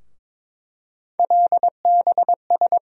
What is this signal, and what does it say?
Morse code, heard as a keyed tone, about 21 wpm, 700 Hz. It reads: LBS
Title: USA Call Sign Postfixes - 3 Letters 22wpm